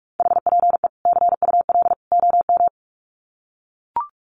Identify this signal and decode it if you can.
Morse code, heard as a keyed tone, about 44 wpm, 700 Hz. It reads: HPE CUL OM
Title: Common QSO Phrases 45wpm